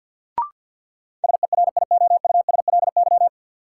Morse code, heard as a keyed tone, about 50 wpm, 700 Hz. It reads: SERIOUSLY
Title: Sentences from Top 1200 Words - New Word Drill Alt - 50wpm